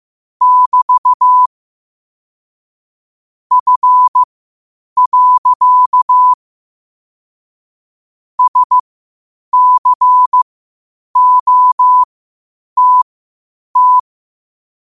Morse code, heard as a keyed tone, about 15 wpm, spaced as for 8 wpm, 1000 Hz. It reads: <BT> F. SCOTT
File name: 8wpm